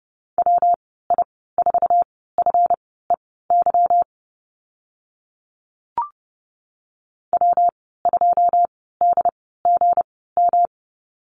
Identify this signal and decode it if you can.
Morse code, heard as a keyed tone, about 30 wpm, spaced as for 17 wpm, 700 Hz. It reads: WI4FEY W2DGM
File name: Lesson 37g - Letter x - Call Signs 30wpm